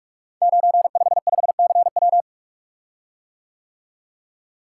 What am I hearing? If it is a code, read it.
Morse code, heard as a keyed tone, about 44 wpm, 690 Hz. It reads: 955CW
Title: Code Groups - 5 Letter:Numbers - 45wpm